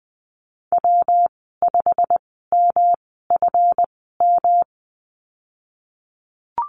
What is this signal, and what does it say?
Morse code, heard as a keyed tone, about 20 wpm, spaced as for 15 wpm, 700 Hz. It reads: W5MFM
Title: 140 Lesson 23g - Letter f - Call Signs 20wpm